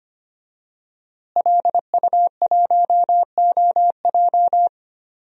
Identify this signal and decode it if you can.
Morse code, heard as a keyed tone, about 25 wpm, 700 Hz. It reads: LU1OJ
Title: DXCC-100 Call Signs - 5 Characters - Repeated 2x - 25wpm